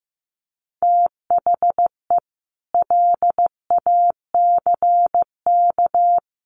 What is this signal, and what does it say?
Morse code, heard as a keyed tone, about 15 wpm, 700 Hz. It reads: THE LACK
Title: Sets of 3 Words 15wpm